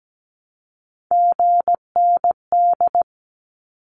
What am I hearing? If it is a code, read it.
Morse code, heard as a keyed tone, about 17 wpm, 700 Hz. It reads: GND
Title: Abbreviations Repeated 3x - 17wpm